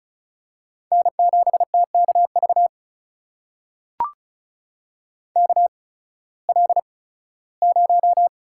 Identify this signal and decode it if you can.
Morse code, heard as a keyed tone, about 36 wpm, spaced as for 19 wpm, 700 Hz. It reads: N7TKV KL0
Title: USA Call Signs - 5 Characters - 8x Character Spacing - 35wpm